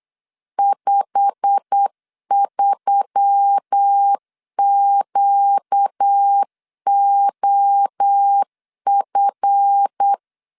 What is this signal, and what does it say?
Morse code, heard as a keyed tone, about 8 wpm, 800 Hz. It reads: 53QOF
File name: CW / DET / 1 9 1500C S C 8wpm 5653